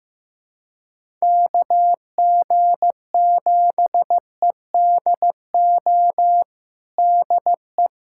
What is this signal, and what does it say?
Morse code, heard as a keyed tone, about 15 wpm, 700 Hz. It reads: KG7EDO DE
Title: USA Callsign DE Callsign 15wpm